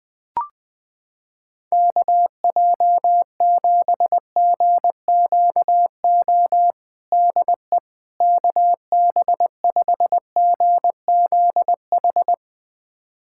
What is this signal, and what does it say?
Morse code, heard as a keyed tone, about 20 wpm, 700 Hz. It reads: KJ7GQO DE KB5GZH
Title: USA Callsign DE Callsign - 6 Characters - 20wpm